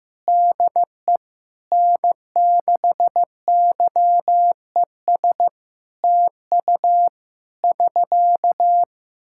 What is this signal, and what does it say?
Morse code, heard as a keyed tone, about 16 wpm, 700 Hz. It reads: DE N6YES TU <SK>
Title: K1USN SST QSO Practice 15wpm